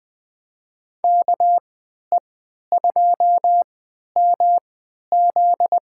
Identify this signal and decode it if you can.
Morse code, heard as a keyed tone, about 21 wpm, spaced as for 12 wpm, 700 Hz. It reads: KE2MZ
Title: USA Call Signs - 3x Character Spacing - 20wpm